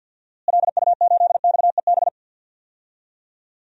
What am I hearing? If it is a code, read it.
Morse code, heard as a keyed tone, about 52 wpm, 700 Hz. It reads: LU8XEB
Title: DXCC-100 Call Signs - 6 Characters - Repeated 2x - 50wpm